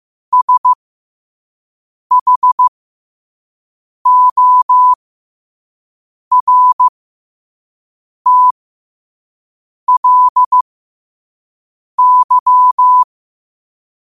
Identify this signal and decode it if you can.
Morse code, heard as a keyed tone, about 15 wpm, spaced as for 5 wpm, 1000 Hz. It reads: SHORTLY